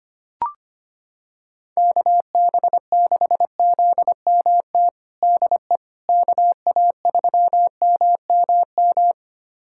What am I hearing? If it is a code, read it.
Morse code, heard as a keyed tone, about 25 wpm, 700 Hz. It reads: KB6ZMT DE KA3MMM
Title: USA Callsign DE Callsign 25wpm